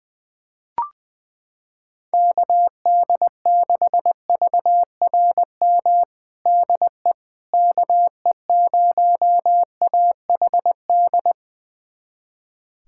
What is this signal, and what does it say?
Morse code, heard as a keyed tone, about 20 wpm, 700 Hz. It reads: KD6VRM DE KE0AHD
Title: USA Callsign DE Callsign 20wpm